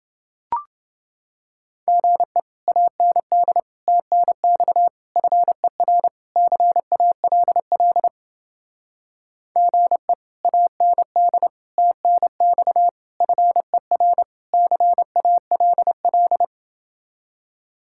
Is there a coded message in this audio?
Yes: Morse code from 1.9 to 16.4 s, heard as a keyed tone, about 30 wpm, 700 Hz. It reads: GE AND TNX FER CALL GE AND TNX FER CALL